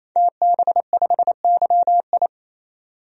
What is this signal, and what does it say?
Morse code, heard as a keyed tone, about 28 wpm, 700 Hz. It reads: TB5YI
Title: DXCC-100 Call Signs - 5 Characters - 28wpm